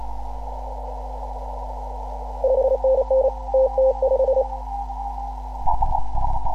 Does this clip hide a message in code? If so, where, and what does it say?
Yes: Morse code from 2.4 to 4.4 s, heard as a keyed tone, about 29 wpm, 530 Hz. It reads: 5NN TT5